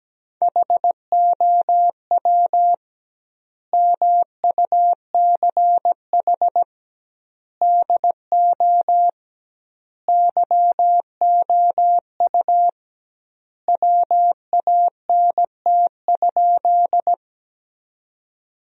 Morse code, heard as a keyed tone, about 17 wpm, 700 Hz. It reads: HOW MUCH DO YOU WANT?
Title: 2x Word-Spacing - Sentences from Top 200 Words - 17wpm